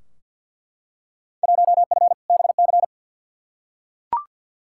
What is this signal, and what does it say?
Morse code, heard as a keyed tone, about 46 wpm, 700 Hz. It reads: 1F BC
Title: Field Day Exchanges Encore - 50wpm